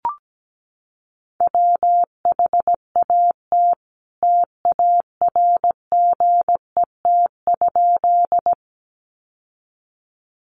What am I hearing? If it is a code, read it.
Morse code, heard as a keyed tone, about 17 wpm, 700 Hz. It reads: WHAT TARGET?
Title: Sentences from Top 1200 Words 17wpm